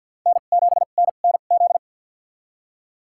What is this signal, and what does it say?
Morse code, heard as a keyed tone, about 47 wpm, spaced as for 35 wpm, 700 Hz. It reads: N7NNZ
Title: USA Call Signs - 2x Character Spacing - 50wpm